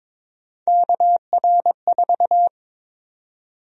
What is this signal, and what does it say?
Morse code, heard as a keyed tone, about 22 wpm, 700 Hz. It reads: KR4
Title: USA Call Sign Prefixes - Repeated 2x - 22wpm